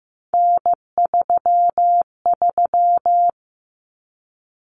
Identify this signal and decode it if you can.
Morse code, heard as a keyed tone, about 15 wpm, 700 Hz. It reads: N33
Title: Common Cut Numbers - 3-Digit 15wpm